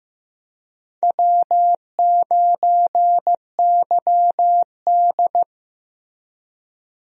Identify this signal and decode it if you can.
Morse code, heard as a keyed tone, about 15 wpm, 700 Hz. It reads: W9YD
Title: USA Call Signs - 4 Characters - Repeated 3x - 15wpm